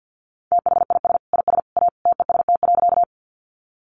Morse code, heard as a keyed tone, about 50 wpm, 700 Hz. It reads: THIS IS A TEST.
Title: Sentences from Top 500 Words 50wpm